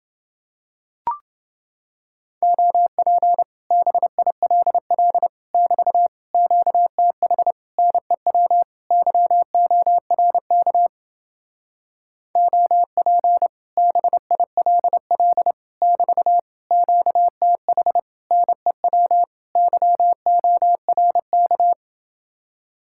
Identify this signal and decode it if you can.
Morse code, heard as a keyed tone, about 28 wpm, 700 Hz. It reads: OP BILL <BT> QTH NEW YORK OP BILL <BT> QTH NEW YORK